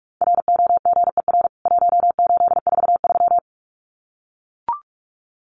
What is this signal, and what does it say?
Morse code, heard as a keyed tone, about 44 wpm, 700 Hz. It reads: ROGER 1843
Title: CWT Exchange 45wpm